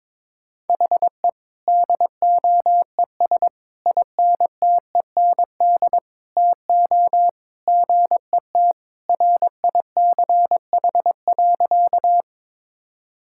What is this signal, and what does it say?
Morse code, heard as a keyed tone, about 22 wpm, 700 Hz. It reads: HE DOES INTEND TO GET RICH.
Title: Sentences from Top 1600 Words 22wpm